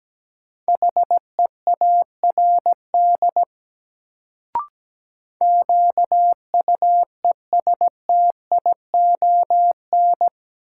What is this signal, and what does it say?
Morse code, heard as a keyed tone, about 17 wpm, 700 Hz. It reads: HEARD QUESTION
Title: Sentences from Top 400 Words - New Word Drill - 17wpm